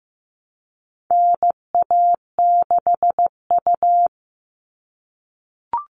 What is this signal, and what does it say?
Morse code, heard as a keyed tone, about 15 wpm, 700 Hz. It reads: NA6U
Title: All Cut Numbers - 4-Digit 15wpm